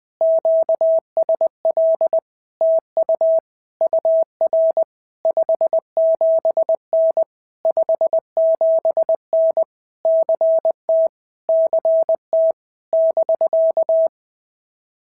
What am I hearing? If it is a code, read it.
Morse code, heard as a keyed tone, about 20 wpm, 650 Hz. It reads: QSL TU UR 57N 57N CT CT <BK>
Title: POTA QSO Practice - 20wpm